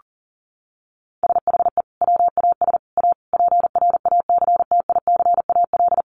Morse code, heard as a keyed tone, about 40 wpm, 700 Hz. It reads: SHE WAS A PRACTICAL
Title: Sentences from Top 1400 Words 40wpm